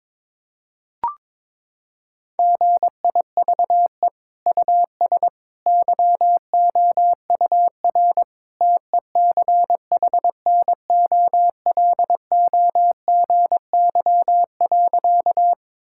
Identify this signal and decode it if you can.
Morse code, heard as a keyed tone, about 22 wpm, 700 Hz. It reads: GIVE US YOUR TECHNOLOGY.
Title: Sentences from Top 1000 Words - Review Part 5 - 22wpm